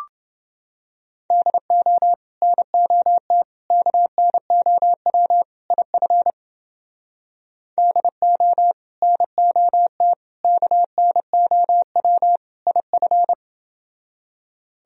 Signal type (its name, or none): Morse code